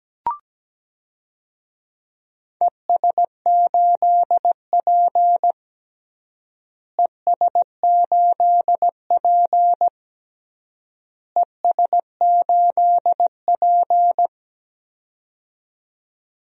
Morse code, heard as a keyed tone, about 17 wpm, 700 Hz. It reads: ES8P ES8P ES8P
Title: DXCC-100 Call Signs - 4 Characters - Repeated 3x - 17wpm